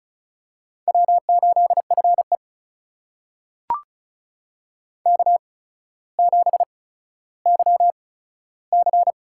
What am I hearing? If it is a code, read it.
Morse code, heard as a keyed tone, about 36 wpm, spaced as for 15 wpm, 700 Hz. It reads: W8FE K7YC
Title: USA Call Signs - 4 Characters - 8x Character Spacing - 35wpm